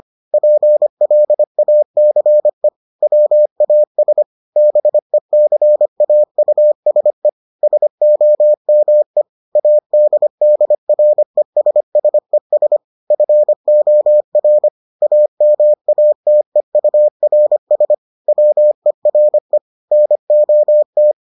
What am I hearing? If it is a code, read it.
Morse code, heard as a keyed tone, about 25 wpm, 600 Hz. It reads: PLACE WAS BECAUSE SOME ADDRESSES FOR AMATEURS WERE NOT